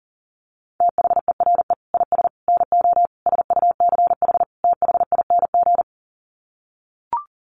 Morse code, heard as a keyed tone, about 40 wpm, 700 Hz. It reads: THERE IS NO SUCH THING
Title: Sets of 5 Words Encore 40wpm